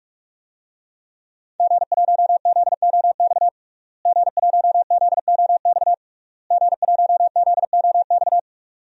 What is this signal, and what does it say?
Morse code, heard as a keyed tone, about 47 wpm, 700 Hz. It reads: G1ZOX G1ZOX G1ZOX
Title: DXCC-100 Call Signs - 5 Characters - Repeated 3x - 45wpm